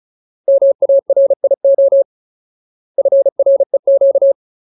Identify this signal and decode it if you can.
Morse code, heard as a keyed tone, about 34 wpm, 550 Hz. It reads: MARIO FREQ
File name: Ultimate Word List / 35WPM Ultimate